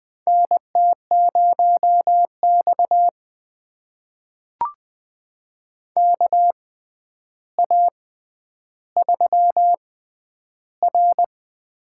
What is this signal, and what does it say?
Morse code, heard as a keyed tone, about 20 wpm, spaced as for 10 wpm, 700 Hz. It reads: NT0X KA3R
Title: USA Call Signs - 4 Characters - 6x Character Spacing - 20wpm